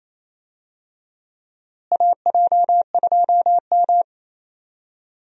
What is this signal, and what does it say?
Morse code, heard as a keyed tone, about 28 wpm, 700 Hz. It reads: AJ2M
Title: USA Call Signs - 4 Characters - Repeated 3x - 28wpm